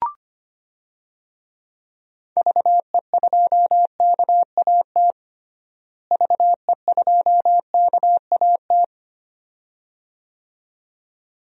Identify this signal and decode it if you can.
Morse code, heard as a keyed tone, about 25 wpm, 700 Hz. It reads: VE2KAT VE2KAT
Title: DXCC-100 Call Signs - 6 Characters - Repeated 2x - 25wpm